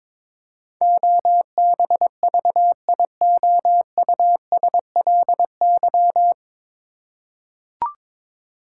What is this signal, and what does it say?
Morse code, heard as a keyed tone, about 22 wpm, 700 Hz. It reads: OBVIOUSLY
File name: 9 Letter Words - First 3 Letters Given - 22wpm